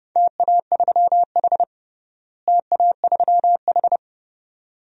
Morse code, heard as a keyed tone, about 30 wpm, 700 Hz. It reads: TA3H TA3H
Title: DXCC-100 Call Signs - 4 Characters - Repeated 2x - 30wpm